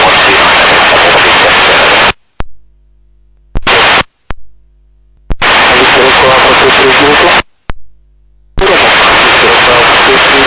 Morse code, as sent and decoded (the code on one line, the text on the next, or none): none